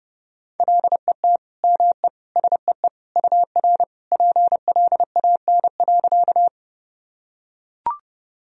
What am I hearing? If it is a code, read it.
Morse code, heard as a keyed tone, about 30 wpm, 700 Hz. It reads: LET ME SEE UR PLAN.
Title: Sentences from Top 500 Words and Abbreviations 30wpm